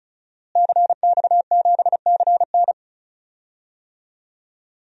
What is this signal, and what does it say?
Morse code, heard as a keyed tone, about 36 wpm, 700 Hz. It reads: CX7CN
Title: DXCC-100 Call Signs - 5 Characters - Repeated 2x - 35wpm